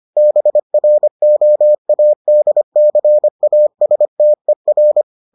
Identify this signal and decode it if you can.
Morse code, heard as a keyed tone, about 25 wpm, 600 Hz. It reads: BROADCASTER